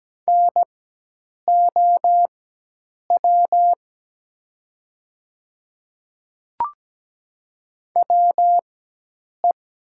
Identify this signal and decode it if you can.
Morse code, heard as a keyed tone, about 17 wpm, spaced as for 8 wpm, 700 Hz. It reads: NOW WE
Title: 4x Character Spacing - Top 100 Words - 17wpm